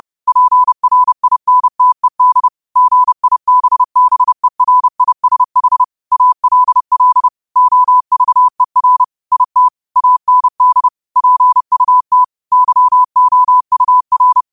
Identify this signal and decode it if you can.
Morse code, heard as a keyed tone, about 30 wpm, 1000 Hz. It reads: PRINTED GIBBERISH ALL OVER IT AND PUT YOUR
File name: qotdfiles / 30wpm